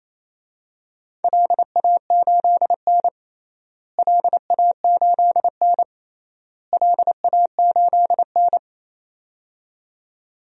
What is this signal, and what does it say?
Morse code, heard as a keyed tone, about 27 wpm, 700 Hz. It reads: LA8N LA8N LA8N
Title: DXCC-100 Call Signs - 4 Characters - Repeated 3x - 28wpm